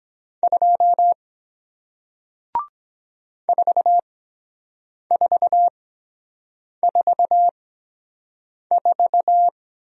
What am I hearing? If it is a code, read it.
Morse code, heard as a keyed tone, about 23 wpm, spaced as for 7 wpm, 700 Hz. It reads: 2 4444